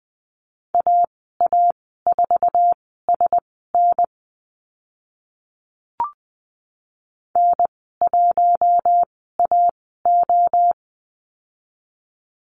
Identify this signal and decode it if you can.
Morse code, heard as a keyed tone, about 20 wpm, spaced as for 14 wpm, 700 Hz. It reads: AA4SN N1AO
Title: 048 Lesson 9g - Number 4 - Call Signs 20wpm